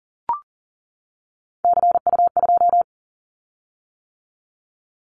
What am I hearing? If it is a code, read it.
Morse code, heard as a keyed tone, about 40 wpm, 700 Hz. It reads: CU2
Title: Callsign Prefixes for Top 100 DXCC - 40wpm